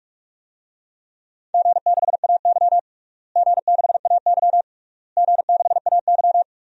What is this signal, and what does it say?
Morse code, heard as a keyed tone, about 42 wpm, 700 Hz. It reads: G6AY G6AY G6AY
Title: DXCC-100 Call Signs - 4 Characters - Repeated 3x - 45wpm